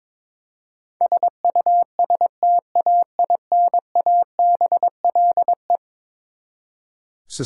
Morse code, heard as a keyed tone, about 22 wpm, 700 Hz. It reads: SUSTAINABLE
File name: practice / 22wpm / Eleven Letter Words - 22wpm